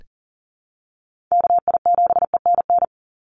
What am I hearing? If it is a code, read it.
Morse code, heard as a keyed tone, about 40 wpm, 700 Hz. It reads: KI7ENN
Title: USA Call Signs 40wpm